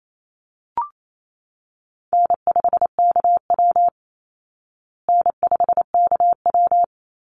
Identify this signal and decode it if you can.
Morse code, heard as a keyed tone, about 28 wpm, 700 Hz. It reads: N5KW N5KW
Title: Call Signs Repeated 3x - 28wpm